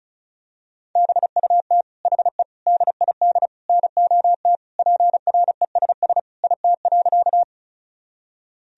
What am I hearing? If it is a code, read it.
Morse code, heard as a keyed tone, about 36 wpm, 700 Hz. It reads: BUT HE DID NOT PRESS IT.
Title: Sentences from Top 1000 Words - Review Part 1 - 35wpm